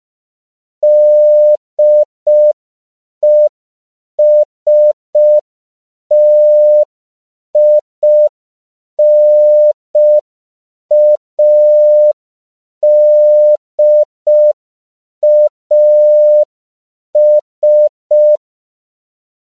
Morse code, heard as a keyed tone, about 5 wpm, 600 Hz. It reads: DESTINADAS
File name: CW / 5PPM